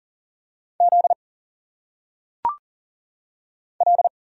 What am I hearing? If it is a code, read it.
Morse code, heard as a keyed tone, about 39 wpm, 700 Hz. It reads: Z L
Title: Single Letters - Phonetic Pronunciation 40wpm